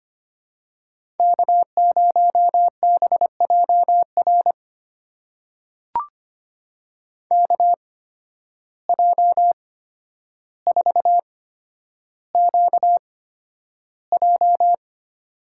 Morse code, heard as a keyed tone, about 25 wpm, spaced as for 11 wpm, 700 Hz. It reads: K0BJR KJ4QJ
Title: USA Call Signs - 5 Characters - 8x Character Spacing - 25wpm